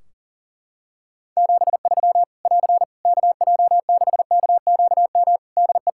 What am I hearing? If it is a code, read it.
Morse code, heard as a keyed tone, about 37 wpm, 700 Hz. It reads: 73 <AR> KJ6KQM DE
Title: K1USN SST QSO Practice 40wpm